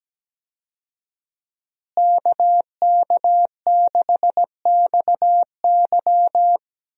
Morse code, heard as a keyed tone, about 17 wpm, 700 Hz. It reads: KK6XY
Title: USA Call Signs - 5 Characters - Repeated 2x - 17wpm